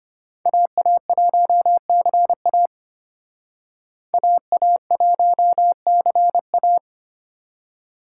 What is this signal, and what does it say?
Morse code, heard as a keyed tone, about 31 wpm, 700 Hz. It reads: AA1CA AA1CA
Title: Speed Racing - USA 2x2 Call Signs 25wpm